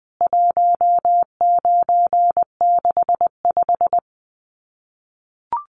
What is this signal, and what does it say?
Morse code, heard as a keyed tone, about 20 wpm, 700 Hz. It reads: 1965
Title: CWT Exchange 20wpm